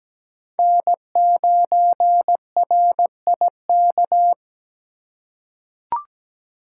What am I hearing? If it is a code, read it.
Morse code, heard as a keyed tone, about 17 wpm, 700 Hz. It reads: N9RIK
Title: USA Call Signs - 5 Characters - 2x Character Spacing - 17wpm